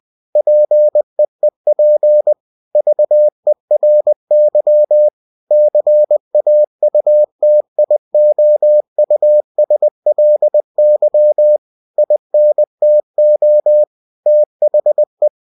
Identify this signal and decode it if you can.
Morse code, heard as a keyed tone, about 20 wpm, 600 Hz. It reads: PEEP VERY CAUTIOUSLY INTO THE
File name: cw-books / winnie-the-pooh / winnie-20wpm-page-0041